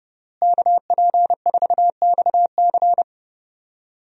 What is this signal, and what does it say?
Morse code, heard as a keyed tone, about 30 wpm, 700 Hz. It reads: KP4XC
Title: DXCC-100 Call Signs - 5 Characters - Repeated 3x - 30wpm